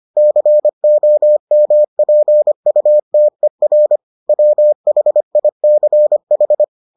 Morse code, heard as a keyed tone, about 25 wpm, 600 Hz. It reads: COMPUTER WHICH